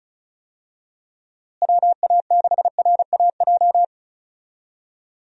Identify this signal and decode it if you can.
Morse code, heard as a keyed tone, about 36 wpm, 700 Hz. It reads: WA6RAJ
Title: USA Call Signs - 6 Characters - Repeated 2x - 35wpm